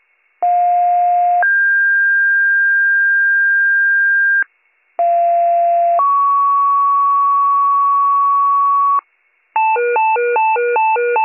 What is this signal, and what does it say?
Two-tone page, heard as a one-second tone, then a three-second tone, about 710 then 1600 Hz, about 690 then 1120 Hz, with noise. It is heard from 0.4 to 9.0 s.